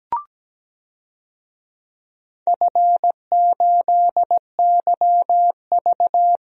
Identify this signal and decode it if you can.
Morse code, heard as a keyed tone, about 17 wpm, 700 Hz. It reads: F8YV
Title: DXCC-100 Call Signs - 4 Characters - Repeated 3x - 17wpm